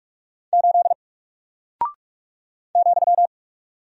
Morse code, heard as a keyed tone, about 44 wpm, 700 Hz. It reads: 8 ,
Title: Single Character (Letter:Number:Punctuation:Prosign) Alt 45wpm